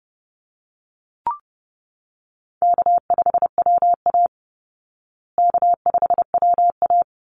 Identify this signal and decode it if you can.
Morse code, heard as a keyed tone, about 30 wpm, 700 Hz. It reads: K5WA K5WA
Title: Call Signs Repeated 2x - 30wpm